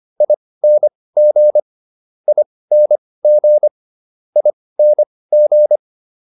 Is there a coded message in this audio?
Yes: Morse code from 0.2 to 5.8 s, heard as a keyed tone, about 25 wpm, spaced as for 18 wpm, 600 Hz. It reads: ING ING ING